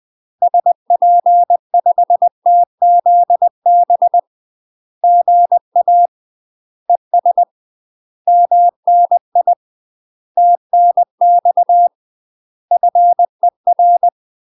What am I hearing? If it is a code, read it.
Morse code, heard as a keyed tone, about 20 wpm, 700 Hz. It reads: SP5TZB GA ES MNI TNX FER
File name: UK MASTER QSOs TRAINING / 20WPM / UK MASTER QSOs 0040